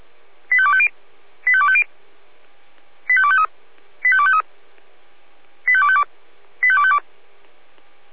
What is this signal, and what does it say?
Five-tone selective call (ZVEI), heard as quick runs of five tones, tones of about 70 ms. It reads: 85250 85250 85253 85253 85252 85252